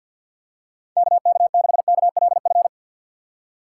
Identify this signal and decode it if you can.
Morse code, heard as a keyed tone, about 52 wpm, 700 Hz. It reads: KK6KLF
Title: USA Call Signs - 6 Characters - 4x Character Spacing - 50wpm